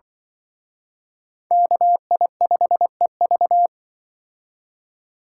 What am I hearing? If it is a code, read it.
Morse code, heard as a keyed tone, about 24 wpm, 700 Hz. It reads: KI5EV
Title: Speed Racing - USA 2x2 Call Signs 20wpm